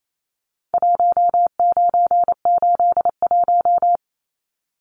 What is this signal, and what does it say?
Morse code, heard as a keyed tone, about 28 wpm, 700 Hz. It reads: 1981
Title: Numbers - Contest Style - 4-Digit 28wpm